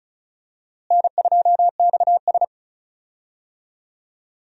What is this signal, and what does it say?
Morse code, heard as a keyed tone, about 36 wpm, 700 Hz. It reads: N2XS
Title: USA Call Signs - 4 Characters - Repeated 2x - 35wpm